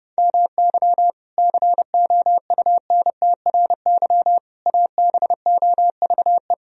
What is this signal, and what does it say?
Morse code, heard as a keyed tone, about 30 wpm, 700 Hz. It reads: MY COUNTRY ABOVE
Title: Sentences from Top 1000 Words - Review Part 2 - 30wpm